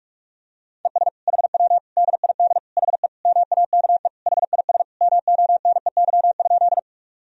Morse code, heard as a keyed tone, about 41 wpm, 700 Hz. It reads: ES HW DID HE MAKE HIS MONEY?
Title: Sentences from Top 400 Words and Abbreviations 45wpm